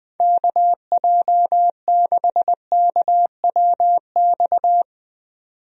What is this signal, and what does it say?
Morse code, heard as a keyed tone, about 20 wpm, 700 Hz. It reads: KJ6KWX
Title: USA Call Signs - 6 Characters - 20wpm